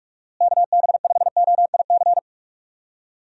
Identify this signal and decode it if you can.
Morse code, heard as a keyed tone, about 43 wpm, 700 Hz. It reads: KB5OIC
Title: USA Call Signs - 6 Characters - Repeated 2x - 45wpm